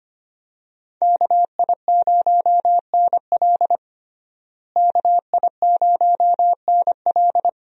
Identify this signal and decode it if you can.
Morse code, heard as a keyed tone, about 25 wpm, 700 Hz. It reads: KI0NL KI0NL
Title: USA Call Signs - 5 Characters - Repeated 3x - 25wpm